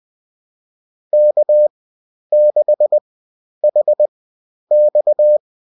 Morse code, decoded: K6HX